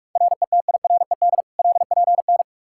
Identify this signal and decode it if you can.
Morse code, heard as a keyed tone, about 47 wpm, 700 Hz. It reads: RETIRED LPN
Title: Practice QSO - Exchange 3 - More Info - 45wpm